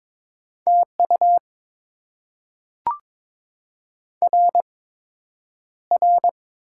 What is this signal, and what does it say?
Morse code, heard as a keyed tone, about 22 wpm, spaced as for 9 wpm, 700 Hz. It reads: TU RR
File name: Top 90 QSO Elements - 8x Character Spacing - 22wpm